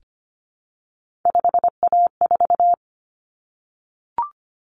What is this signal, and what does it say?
Morse code, heard as a keyed tone, about 25 wpm, 700 Hz. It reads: 5A4
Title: Common Cut Numbers - 3-Digit 25wpm